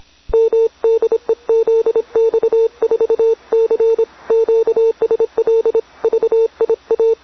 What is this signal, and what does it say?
Morse code, heard as a keyed tone, about 26 wpm, 450 Hz. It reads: M DE ZX4C QSL VIA